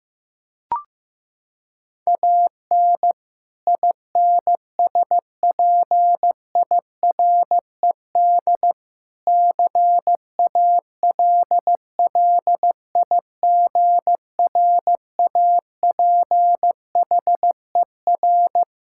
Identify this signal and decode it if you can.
Morse code, heard as a keyed tone, about 15 wpm, 700 Hz. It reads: AN INSPIRED CALLIGRAPHER
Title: Pangram Challenge v2 - 15wpm